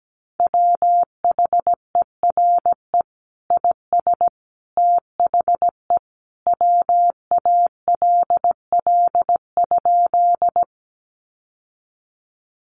Morse code, heard as a keyed tone, about 17 wpm, 700 Hz. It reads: WHERE IS THE WALL?
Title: Sentences from Top 1100 Words 17wpm